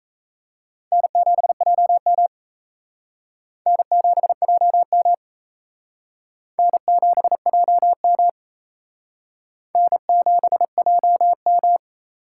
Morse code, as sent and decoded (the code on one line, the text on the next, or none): -. --... .--- -- / -. --... .--- -- / -. --... .--- -- / -. --... .--- --
N7JM N7JM N7JM N7JM